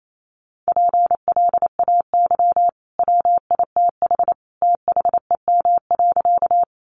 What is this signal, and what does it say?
Morse code, heard as a keyed tone, about 28 wpm, 700 Hz. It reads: PLAY WITH THEM.
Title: Sentences from Top 200 Words 28wpm